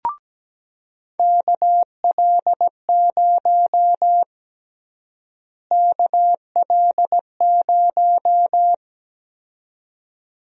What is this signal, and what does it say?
Morse code, heard as a keyed tone, about 17 wpm, 700 Hz. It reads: KL0 KL0
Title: USA Call Sign Prefixes - Repeated 2x - 17wpm